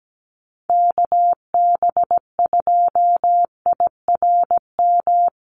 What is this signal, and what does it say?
Morse code, heard as a keyed tone, about 17 wpm, 700 Hz. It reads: KB2IRM
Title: USA Call Signs Repeated 2x 17wpm